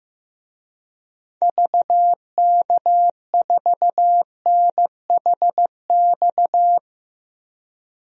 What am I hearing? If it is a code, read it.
Morse code, heard as a keyed tone, about 15 wpm, 700 Hz. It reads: VK4NHX
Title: DXCC-100 Call Signs - 6 Characters - 15wpm